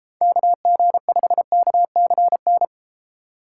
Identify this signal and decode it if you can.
Morse code, heard as a keyed tone, about 33 wpm, 700 Hz. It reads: KG5KCN